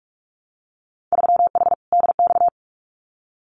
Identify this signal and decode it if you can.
Morse code, heard as a keyed tone, about 46 wpm, 700 Hz. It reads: 3H DX
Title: Winter Field Day Exchanges Encore 45wpm